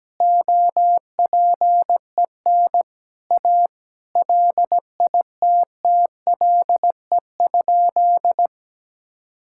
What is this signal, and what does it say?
Morse code, heard as a keyed tone, about 17 wpm, 700 Hz. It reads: OPEN A LITTLE?
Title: Sentences from Top 1600 Words 17wpm